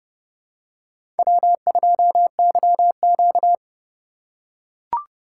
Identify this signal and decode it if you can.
Morse code, heard as a keyed tone, about 28 wpm, 700 Hz. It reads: W2YQ